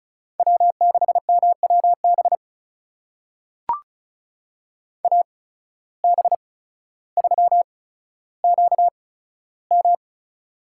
Morse code, heard as a keyed tone, about 34 wpm, spaced as for 15 wpm, 700 Hz. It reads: W6MWB AB3QM
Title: USA Call Signs - 5 Characters - 8x Character Spacing - 35wpm